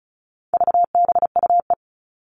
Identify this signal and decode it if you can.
Morse code, heard as a keyed tone, about 34 wpm, 700 Hz. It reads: VBUE
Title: All Cut Numbers - 4-Digit 35wpm